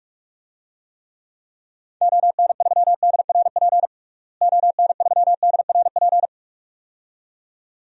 Morse code, heard as a keyed tone, about 44 wpm, 700 Hz. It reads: ON3DRP ON3DRP
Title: DXCC-100 Call Signs - 6 Characters - Repeated 2x - 45wpm